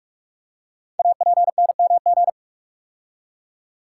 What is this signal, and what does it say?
Morse code, heard as a keyed tone, about 44 wpm, 700 Hz. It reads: APNMG
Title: Code Groups - 5 Letters - 45wpm